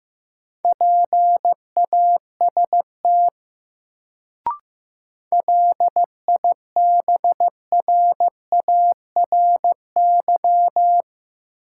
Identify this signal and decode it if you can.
Morse code, heard as a keyed tone, about 15 wpm, 700 Hz. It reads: PAST LIBRARY
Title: Sentences from Top 600 Words - New Word Drill - 15wpm